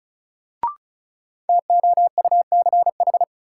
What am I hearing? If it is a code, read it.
Morse code, heard as a keyed tone, about 36 wpm, 700 Hz. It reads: TOUCH
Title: Sentences from Top 1200 Words - New Word Drill - 35wpm